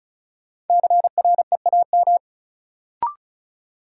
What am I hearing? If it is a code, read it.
Morse code, heard as a keyed tone, about 36 wpm, 700 Hz. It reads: CREAM
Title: Sentences from Top 1500 Words - New Word Drill Alt - 35wpm